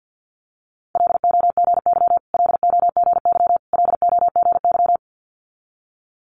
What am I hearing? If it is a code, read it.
Morse code, heard as a keyed tone, about 50 wpm, 690 Hz. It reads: LOGY LOGY LOGY
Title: Multi-letter Phonemes Repeated 3x 50wpm